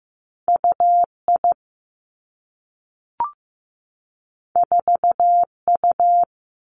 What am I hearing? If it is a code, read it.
Morse code, heard as a keyed tone, about 15 wpm, 700 Hz. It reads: UI 4U
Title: Two Challenging Letter-Number Combinations 15wpm